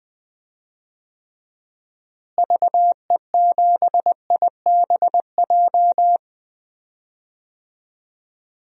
Morse code, heard as a keyed tone, about 20 wpm, 700 Hz. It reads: VE7IBJ
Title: DXCC-100 Call Signs - 6 Characters - Repeated 3x - 20wpm